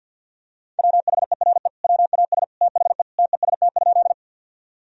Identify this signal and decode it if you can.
Morse code, heard as a keyed tone, about 48 wpm, 700 Hz. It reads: WHERE WAS THE TEST?
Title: Sentences from Top 1000 Words - Review Part 3 - 50wpm